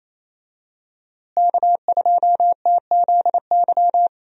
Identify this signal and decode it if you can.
Morse code, heard as a keyed tone, about 28 wpm, 700 Hz. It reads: K2TZY
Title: USA Call Signs - 5 Characters - Repeated 3x - 28wpm